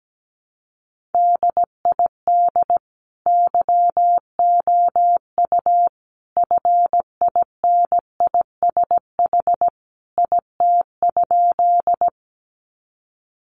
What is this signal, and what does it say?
Morse code, heard as a keyed tone, about 17 wpm, 700 Hz. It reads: DID YOU FINISH IT?
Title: Sentences from Top 1200 Words 17wpm